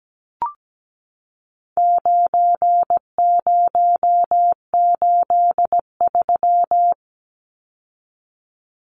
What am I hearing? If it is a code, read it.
Morse code, heard as a keyed tone, about 17 wpm, 700 Hz. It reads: 9083
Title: Numbers - 4-Digit 17wpm